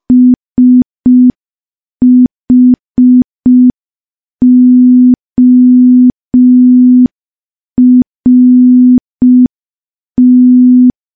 Morse code, heard as a keyed tone, about 5 wpm, 250 Hz. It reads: SHORT
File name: morse / morse-training-05WPM